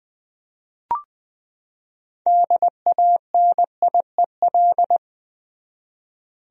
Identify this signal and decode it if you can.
Morse code, heard as a keyed tone, about 20 wpm, 700 Hz. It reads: DANIEL
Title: USA Common Names - 20wpm